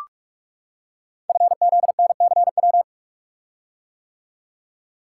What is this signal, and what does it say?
Morse code, heard as a keyed tone, about 44 wpm, 690 Hz. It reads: FZNCW
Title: Code Groups - 5 Letters - 45wpm